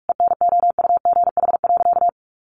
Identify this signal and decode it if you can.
Morse code, heard as a keyed tone, about 46 wpm, 700 Hz. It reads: ENOUGH.
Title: Sentences from Top 400 Words 45wpm